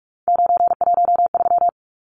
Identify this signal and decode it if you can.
Morse code, heard as a keyed tone, about 44 wpm, 700 Hz. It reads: 913
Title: Numbers - 3-Digit 45wpm